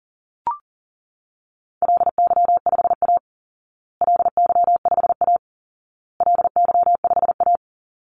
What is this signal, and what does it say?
Morse code, heard as a keyed tone, about 40 wpm, 700 Hz. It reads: LY5A LY5A LY5A
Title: Call Signs Repeated 2x - 40wpm